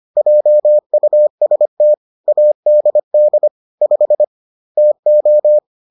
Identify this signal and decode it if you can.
Morse code, heard as a keyed tone, about 25 wpm, 600 Hz. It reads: JUST ADD 5 TO